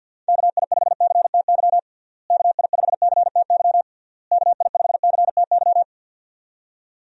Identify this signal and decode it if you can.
Morse code, heard as a keyed tone, about 52 wpm, 700 Hz. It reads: KI5CTY KI5CTY KI5CTY
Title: USA Call Signs - 6 Characters - Repeated 3x - 50wpm